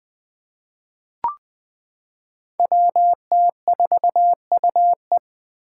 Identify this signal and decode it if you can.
Morse code, heard as a keyed tone, about 20 wpm, 700 Hz. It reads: WT4UE
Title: Five Challenging Letter-Number Combinations 20wpm